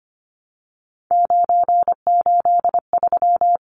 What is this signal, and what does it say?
Morse code, heard as a keyed tone, about 25 wpm, 690 Hz. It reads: 983
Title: Numbers - 3-Digit 25wpm